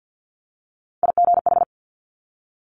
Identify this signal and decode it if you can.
Morse code, heard as a keyed tone, about 50 wpm, 700 Hz. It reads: IGH
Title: Top Three Letter Combinations in Words 50wpm